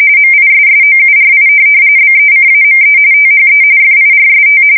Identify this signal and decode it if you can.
RTTY, about 45.45 baud. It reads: MISSOURI BECAME THE FIRST
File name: fsk170 / USS-Missouri